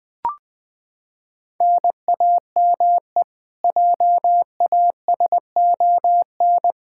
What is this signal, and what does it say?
Morse code, heard as a keyed tone, about 20 wpm, 700 Hz. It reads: NAME JASON
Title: Practice QSO - Exchange 1 - Beginning - 20wpm